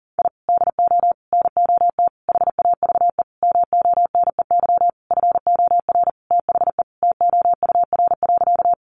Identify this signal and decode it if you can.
Morse code, heard as a keyed tone, about 37 wpm, 700 Hz. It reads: I DO NOT HAVE MONEY FOR THE TOUR.
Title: Sentences from Top 1400 Words 40wpm